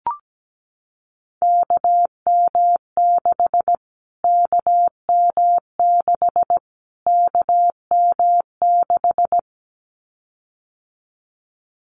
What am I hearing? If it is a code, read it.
Morse code, heard as a keyed tone, about 17 wpm, 700 Hz. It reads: KM6 KM6 KM6
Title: Callsign Prefixes for Top 25 DXCC - Repeated 3x - 17wpm